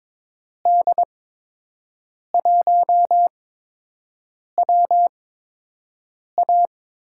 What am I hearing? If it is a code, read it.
Morse code, heard as a keyed tone, about 22 wpm, spaced as for 6 wpm, 700 Hz. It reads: D1WA
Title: USA Call Signs - 5 Characters - 8x Character Spacing - 22wpm